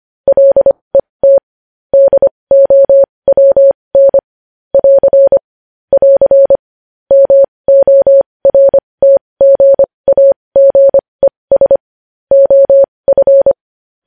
Morse code, heard as a keyed tone, about 25 wpm, spaced as for 20 wpm, 550 Hz. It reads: LET DOWN <AR> <AR> MORTGAGES OF
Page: news headlines in morse code at 20 wpm